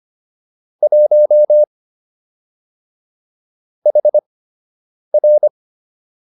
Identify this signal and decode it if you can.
Morse code, heard as a keyed tone, about 25 wpm, spaced as for 8 wpm, 600 Hz. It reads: 1 HR